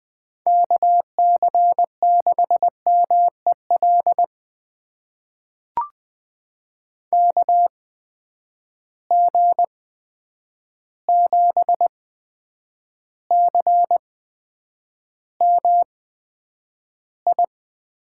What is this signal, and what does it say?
Morse code, heard as a keyed tone, about 20 wpm, spaced as for 9 wpm, 700 Hz. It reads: KC6MEL KG7CMI
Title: USA Call Signs - 6 Characters - 8x Character Spacing - 20wpm